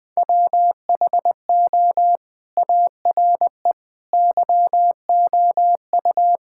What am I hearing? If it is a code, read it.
Morse code, heard as a keyed tone, about 20 wpm, 700 Hz. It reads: WHO ARE YOU
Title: Sentences with 2-Word Phrases - 20wpm